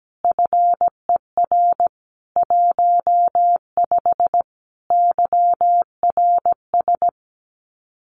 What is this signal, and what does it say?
Morse code, heard as a keyed tone, about 17 wpm, 700 Hz. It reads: FER 15 YRS
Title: Common QSO Phrases Encore 17wpm